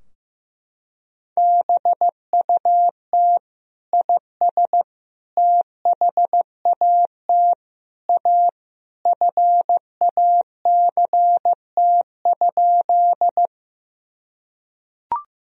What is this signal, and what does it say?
Morse code, heard as a keyed tone, about 16 wpm, 700 Hz. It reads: BUT IS THAT A FACT?
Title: Sentences from Top 1000 Words - Review Part 3 - 15wpm